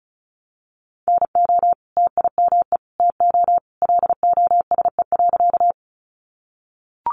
Morse code, heard as a keyed tone, about 34 wpm, 700 Hz. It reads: NO TIME TO LOSE.
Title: Sentences from Top 1100 Words 35wpm